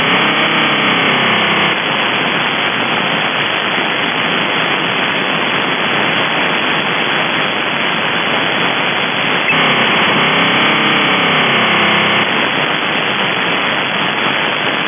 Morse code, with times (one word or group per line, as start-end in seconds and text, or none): none